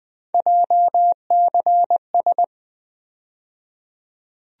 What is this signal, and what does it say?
Morse code, heard as a keyed tone, about 21 wpm, 700 Hz. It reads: JCS
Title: USA Call Sign Postfixes - 3 Letters 20wpm